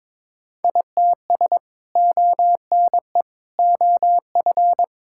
Morse code, heard as a keyed tone, about 22 wpm, 700 Hz. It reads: ITS ONE OF
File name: practice / 22wpm / Sentences with 3-Word Phrases - 22wpm